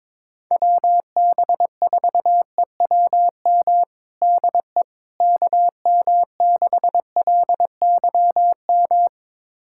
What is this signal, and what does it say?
Morse code, heard as a keyed tone, about 21 wpm, 700 Hz. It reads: WB4EWM DE KM6LYM
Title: USA Callsign DE Callsign 22wpm